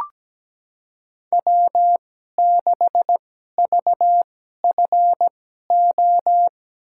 Morse code, heard as a keyed tone, about 17 wpm, spaced as for 12 wpm, 700 Hz. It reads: W6VFO
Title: USA Call Signs - 5 Characters - 2x Character Spacing - 17wpm